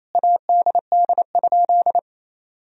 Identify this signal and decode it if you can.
Morse code, heard as a keyed tone, about 28 wpm, 700 Hz. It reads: ADD?
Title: Sentences from Top 1000 Words - Review Part 1 - 28wpm